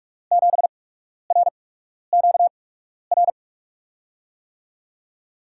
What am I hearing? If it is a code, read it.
Morse code, heard as a keyed tone, about 44 wpm, spaced as for 12 wpm, 700 Hz. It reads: 7RQR
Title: USA Call Signs - 6 Characters - 8x Character Spacing - 45wpm